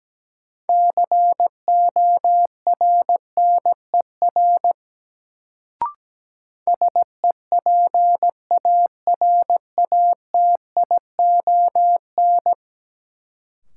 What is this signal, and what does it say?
Morse code, heard as a keyed tone, about 17 wpm, 700 Hz. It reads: CORNER SEPARATION
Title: Sets of 5 Words - New Word Drill - 17wpm